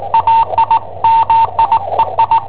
Morse code, heard as a keyed tone, about 19 wpm, 910 Hz, with radio noise. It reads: AI MIEI
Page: Cassetta esercizi preparata da I7OHP per IW5ELP